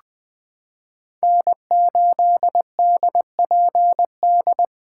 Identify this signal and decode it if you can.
Morse code, heard as a keyed tone, about 21 wpm, 700 Hz. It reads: N8DPD
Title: K1USN SST QSO Practice - 20wpm